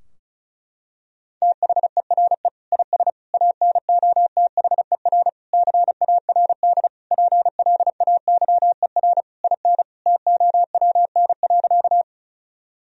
Morse code, heard as a keyed tone, about 36 wpm, 700 Hz. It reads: THERE IS ANOTHER CARD PLAYER IN TOWN.
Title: Sentences from Top 1000 Words - Review Part 4 - 35wpm